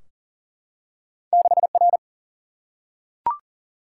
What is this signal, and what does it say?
Morse code, heard as a keyed tone, about 41 wpm, 700 Hz. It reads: 6 R
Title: Single-Letter Single-Number 40wpm